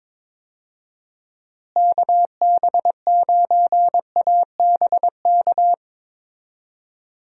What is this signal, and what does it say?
Morse code, heard as a keyed tone, about 22 wpm, 700 Hz. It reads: KB9ABK
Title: USA Call Signs - 6 Characters - Repeated 3x - 22wpm